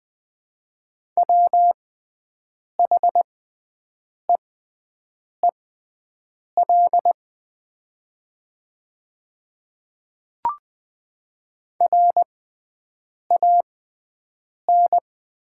Morse code, heard as a keyed tone, about 20 wpm, spaced as for 7 wpm, 700 Hz. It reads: WHEEL RAN
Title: 6x Character Spacing - Top 500 Words - 20wpm